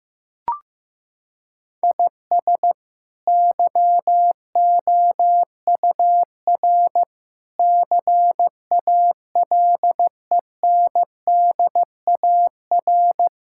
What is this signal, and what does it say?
Morse code, heard as a keyed tone, about 15 wpm, 700 Hz. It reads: IS YOUR CALENDAR
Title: Sentences from Top 1600 Words 15wpm